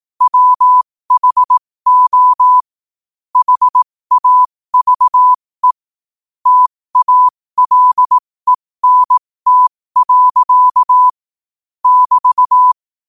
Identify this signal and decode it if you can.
Morse code, heard as a keyed tone, about 18 wpm, spaced as for 15 wpm, 1000 Hz. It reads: WHO HAVE TALENT. <BT>
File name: qotdfiles / 15wpm